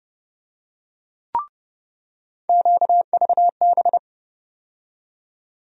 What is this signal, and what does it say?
Morse code, heard as a keyed tone, about 31 wpm, 700 Hz. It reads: QVB
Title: USA Call Sign Postfixes - 3 Letters 30wpm